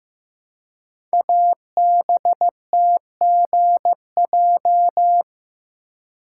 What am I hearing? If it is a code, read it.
Morse code, heard as a keyed tone, about 15 wpm, 700 Hz. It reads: ABTGJ
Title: Code Groups - 5 Letters - 15wpm